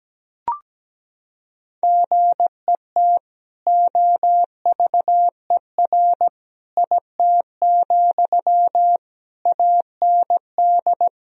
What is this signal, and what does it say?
Morse code, heard as a keyed tone, about 17 wpm, 700 Hz. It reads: GET OVER IT, AND
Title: Sentences from Top 1000 Words - Review Part 2 - 17wpm